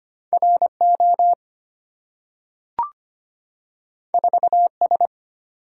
Morse code, heard as a keyed tone, about 25 wpm, 700 Hz. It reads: RO 4S
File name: Two Letter-Number Combinations - 25wpm